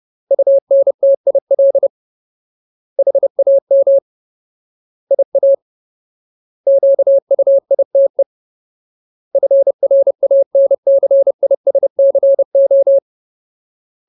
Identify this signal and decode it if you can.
Morse code, heard as a keyed tone, about 30 wpm, 550 Hz. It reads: UNTIL HAM IA QUITE FRANCISCO
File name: Ultimate Word List / 30WPM Ultimate